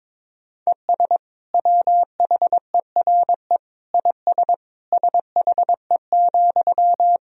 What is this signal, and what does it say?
Morse code, heard as a keyed tone, about 22 wpm, 700 Hz. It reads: ES WHERE IS SHE,
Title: Sentences from Top 200 Words and Abbreviations 22wpm